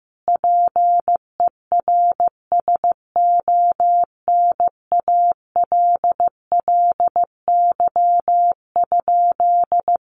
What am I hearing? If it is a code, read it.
Morse code, heard as a keyed tone, about 15 wpm, 700 Hz. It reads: PERSONALLY?
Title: Sentences from Top 1400 Words 15wpm